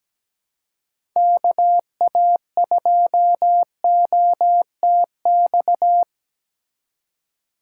Morse code, heard as a keyed tone, about 17 wpm, 700 Hz. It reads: KA2OTX
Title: USA Call Signs - 2x Character Spacing - 17wpm